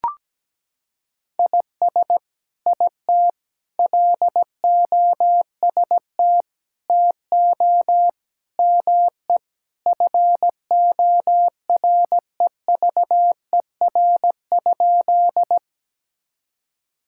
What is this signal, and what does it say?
Morse code, heard as a keyed tone, about 17 wpm, 700 Hz. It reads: IS IT LOST TO ME FOREVER?
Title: Sentences from Top 1600 Words 17wpm